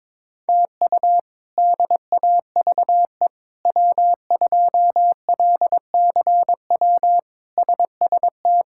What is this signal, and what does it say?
Morse code, heard as a keyed tone, about 22 wpm, 700 Hz. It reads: TU DAVE W2LCW SST
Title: K1USN SST QSO Practice - 22wpm